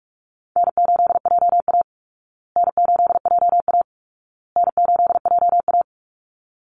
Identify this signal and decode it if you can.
Morse code, heard as a keyed tone, about 44 wpm, 700 Hz. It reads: N8JA N8JA N8JA
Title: Call Signs Repeated 2x - 45wpm